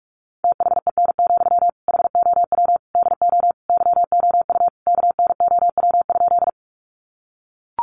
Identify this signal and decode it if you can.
Morse code, heard as a keyed tone, about 44 wpm, 700 Hz. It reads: THEN, HOW DO YOU KNOW?
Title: Sentences from Top 100 Words 45wpm